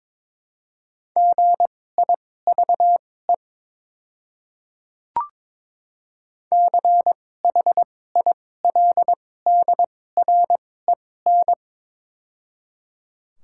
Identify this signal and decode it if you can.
Morse code, heard as a keyed tone, about 22 wpm, spaced as for 16 wpm, 700 Hz. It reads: GIVE CHILDREN
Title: 2x Character Spacing - Top 300 Words - 22wpm